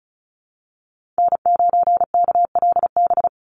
Morse code, heard as a keyed tone, about 34 wpm, 700 Hz. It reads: N9KLB
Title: USA Call Signs 35wpm